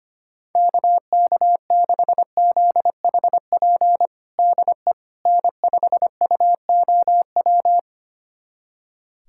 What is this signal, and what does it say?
Morse code, heard as a keyed tone, about 25 wpm, 700 Hz. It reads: KK6ZHP DE N5UOW
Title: USA Callsign DE Callsign 25wpm